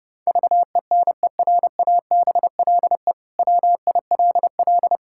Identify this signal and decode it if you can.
Morse code, heard as a keyed tone, about 30 wpm, 700 Hz. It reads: VENERABLE WILL
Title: Pangram Challenge v2 - 30wpm